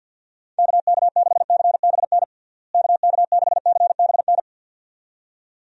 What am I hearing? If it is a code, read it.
Morse code, heard as a keyed tone, about 52 wpm, 700 Hz. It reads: KK6CBN KK6CBN
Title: USA Call Signs - 6 Characters - Repeated 2x - 50wpm